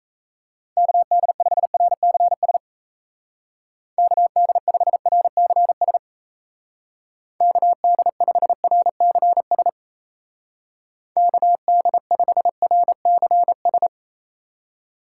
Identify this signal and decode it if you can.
Morse code, heard as a keyed tone, about 37 wpm, 700 Hz. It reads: KD5RCS KD5RCS KD5RCS KD5RCS